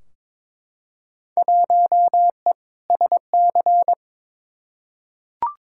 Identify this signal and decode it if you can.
Morse code, heard as a keyed tone, about 22 wpm, 700 Hz. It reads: 1E SC
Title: Field Day Exchanges - 22wpm